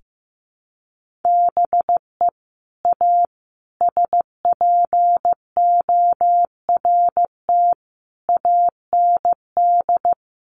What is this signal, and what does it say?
Morse code, heard as a keyed tone, about 15 wpm, 700 Hz. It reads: BE A SPORT AND
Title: Sentences from Top 1500 Words 15wpm